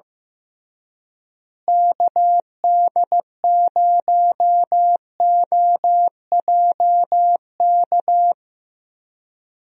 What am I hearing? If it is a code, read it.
Morse code, heard as a keyed tone, about 15 wpm, 700 Hz. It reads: KD0OJK
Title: USA Call Signs - 6 Characters - Repeated 3x - 15wpm